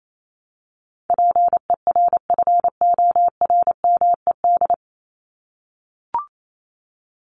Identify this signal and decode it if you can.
Morse code, heard as a keyed tone, about 28 wpm, 700 Hz. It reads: PERFORMED
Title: ED Suffix - Top 100 Words with Base Words 28wpm